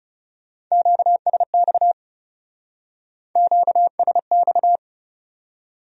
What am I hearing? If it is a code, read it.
Morse code, heard as a keyed tone, about 34 wpm, 700 Hz. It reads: QSX QSX
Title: Speed-Racing Q-Codes - 30wpm